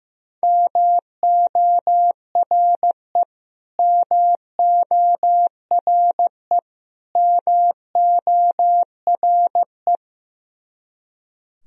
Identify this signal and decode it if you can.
Morse code, heard as a keyed tone, about 15 wpm, 700 Hz. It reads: MORE MORE MORE
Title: Top 100 Words Repeated 15wpm